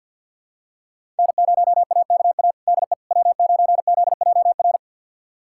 Morse code, heard as a keyed tone, about 52 wpm, 700 Hz. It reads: N0AKA DE W9ZJR
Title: USA Callsign DE Callsign - 5 Characters - 50wpm